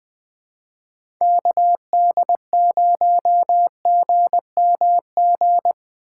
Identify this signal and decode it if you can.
Morse code, heard as a keyed tone, about 20 wpm, 700 Hz. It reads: KD0GMG
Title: USA Call Signs - 6 Characters - 2x Character Spacing - 20wpm